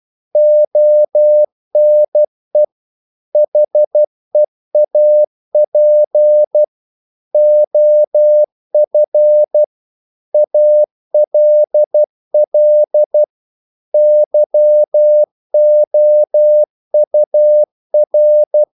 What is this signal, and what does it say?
Morse code, heard as a keyed tone, about 12 wpm, 600 Hz. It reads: ONE HEAP OF ALL YOUR
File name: CW / Practice / If12